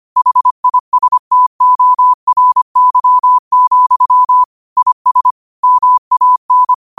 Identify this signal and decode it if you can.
Morse code, heard as a keyed tone, about 25 wpm, 1000 Hz. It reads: HISTORY, IS MAN
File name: qotdfiles / 25wpm